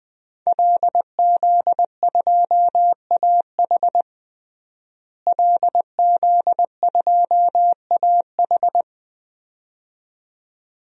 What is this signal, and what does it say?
Morse code, heard as a keyed tone, about 20 wpm, 700 Hz. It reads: LZ2AH LZ2AH
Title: DXCC-100 Call Signs - 5 Characters - Repeated 2x - 20wpm